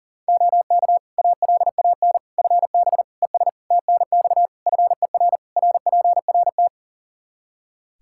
Morse code, heard as a keyed tone, about 40 wpm, 700 Hz. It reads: OK ALAN FB ES TNX FER RPRT
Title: Practice QSO - Exchange 2 - Info - 40wpm